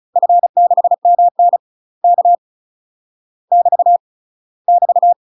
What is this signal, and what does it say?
Morse code, heard as a keyed tone, about 36 wpm, 700 Hz. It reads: F6MN K <BT> <BT>
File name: UK MASTER QSOs TRAINING / 35WPM / UK MASTER QSOs 0031